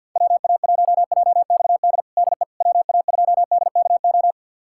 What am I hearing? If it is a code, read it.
Morse code, heard as a keyed tone, about 48 wpm, 690 Hz. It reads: WA1JXD DE WA2DKO
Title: USA Callsign DE Callsign - 6 Characters - 50wpm